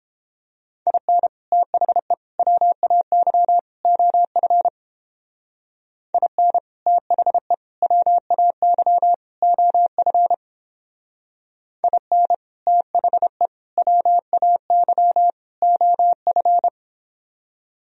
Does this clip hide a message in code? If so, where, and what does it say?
Yes: Morse code from 0.9 to 16.7 s, heard as a keyed tone, about 29 wpm, 700 Hz. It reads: IN THE WAY OF IN THE WAY OF IN THE WAY OF